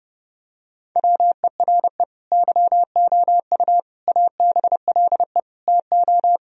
Morse code, heard as a keyed tone, about 32 wpm, 700 Hz. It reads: WERE YOU ABLE TO
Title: Sentences from Top 1000 Words - Review Part 2 - 30wpm